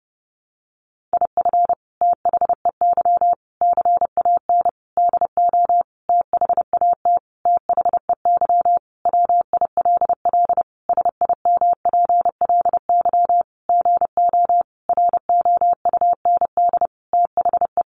Morse code, heard as a keyed tone, about 30 wpm, 700 Hz. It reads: IF THEY CAN DO THAT THEY WILL SIMPLY GO ROUND THE
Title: Sentences from Top 600 Words 30wpm